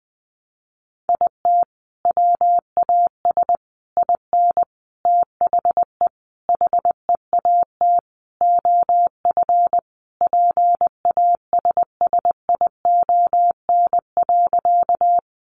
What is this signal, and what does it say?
Morse code, heard as a keyed tone, about 20 wpm, 700 Hz. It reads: IT WAS IN THE HEAT OF PASSION.
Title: Sentences from Top 1300 Words 20wpm